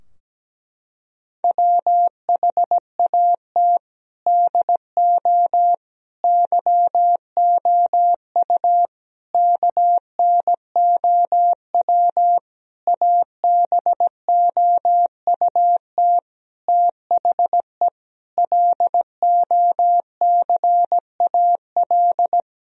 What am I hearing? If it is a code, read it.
Morse code, heard as a keyed tone, about 17 wpm, 700 Hz. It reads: WHAT DO YOU KNOW ABOUT THE LOCAL
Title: Sentences from Top 1000 Words - Review Part 3 - 17wpm